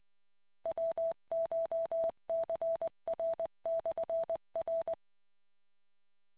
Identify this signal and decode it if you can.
Morse code, heard as a keyed tone, about 20 wpm, 660 Hz. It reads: WCHCR/R